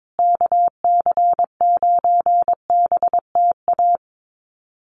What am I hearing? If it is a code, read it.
Morse code, heard as a keyed tone, about 22 wpm, 700 Hz. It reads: KC9BTA
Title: USA Call Signs Repeated 2x 22wpm